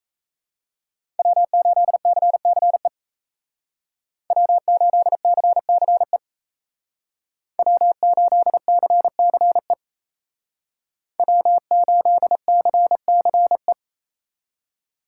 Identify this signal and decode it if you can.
Morse code, heard as a keyed tone, about 33 wpm, 700 Hz. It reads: W8CCE W8CCE W8CCE W8CCE
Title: Speed Racing - USA 1x3 Call Signs 28wpm